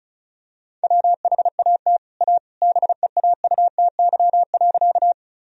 Morse code, heard as a keyed tone, about 34 wpm, 700 Hz. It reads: WHAT A BEAUTY.
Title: Sentences from Top 1000 Words - Review Part 2 - 35wpm